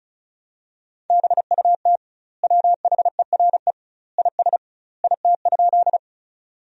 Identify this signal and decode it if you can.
Morse code, heard as a keyed tone, about 36 wpm, 700 Hz. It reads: BUT WHERE IS IT?
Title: 2x Word-Spacing - Sentences from Top 200 Words - 35wpm